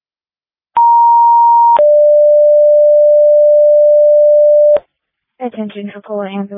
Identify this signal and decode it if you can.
Two-tone page, heard as a one-second tone, then a three-second tone, about 940 then 590 Hz, with speech. It is heard from 0.8 to 4.8 s.
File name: Tripoli Ambulance 2025 03 06 15 36 10